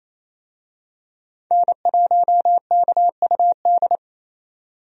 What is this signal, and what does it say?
Morse code, heard as a keyed tone, about 27 wpm, 700 Hz. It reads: N1KUD
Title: USA Call Signs - 5 Characters - 28wpm